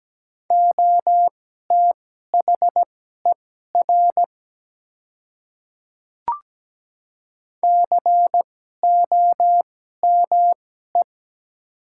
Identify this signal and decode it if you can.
Morse code, heard as a keyed tone, about 17 wpm, spaced as for 12 wpm, 700 Hz. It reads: OTHER COME
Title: 2x Character Spacing - Top 100 Words - 17wpm